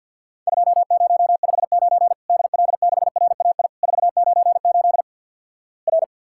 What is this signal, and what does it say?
Morse code, heard as a keyed tone, about 50 wpm, 700 Hz. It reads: 2059 DL6RAI 498 R
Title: Worked All Europe DX Contest - QTC Exchange with Repeats - 50wpm